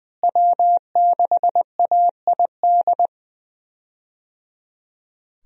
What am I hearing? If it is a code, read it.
Morse code, heard as a keyed tone, about 20 wpm, 700 Hz. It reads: W6AID
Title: USA Call Signs - 5 Characters - Repeated 3x - 20wpm